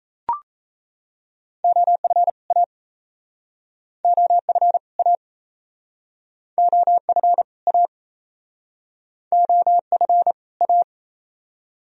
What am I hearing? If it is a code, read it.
Morse code, heard as a keyed tone, about 37 wpm, 700 Hz. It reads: OF A OF A OF A OF A